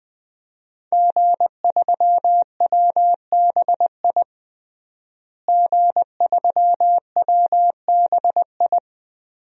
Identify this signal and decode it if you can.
Morse code, heard as a keyed tone, about 20 wpm, 700 Hz. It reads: G3WBI G3WBI
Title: DXCC-100 Call Signs - 5 Characters - Repeated 3x - 20wpm